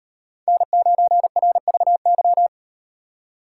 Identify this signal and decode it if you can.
Morse code, heard as a keyed tone, about 38 wpm, 700 Hz. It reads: N9RVY